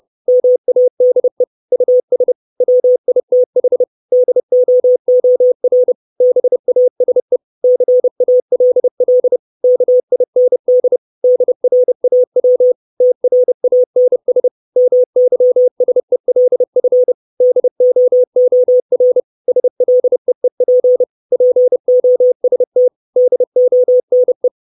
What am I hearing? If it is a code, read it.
Morse code, heard as a keyed tone, about 30 wpm, 500 Hz. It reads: MADE US WITH DOOR BASE CALL KIND DRAW TRANS MYSELF DOOR SLEEP POST DONE
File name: 632 Words 30WPM